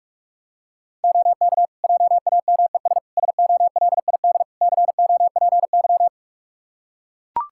Morse code, heard as a keyed tone, about 41 wpm, 700 Hz. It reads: OK JAMES SOLID COPY
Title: Practice QSO - Exchange 3 - More Info - 45wpm